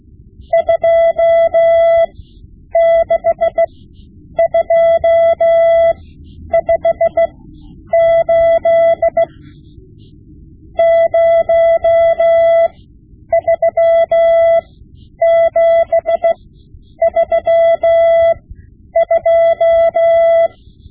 Morse code, heard as a keyed tone, about 16 wpm, spaced as for 9 wpm, 670 Hz. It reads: 26258 03732
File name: Morse Lessons / Morse Lessons / Tape-5B